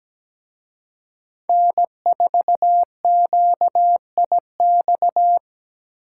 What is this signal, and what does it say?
Morse code, heard as a keyed tone, about 20 wpm, 700 Hz. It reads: N4QIX
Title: Speed Racing - USA 1x3 Call Signs 17wpm